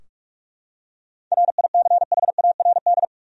Morse code, heard as a keyed tone, about 44 wpm, 700 Hz. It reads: RICHARD
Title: USA Common Names - 45wpm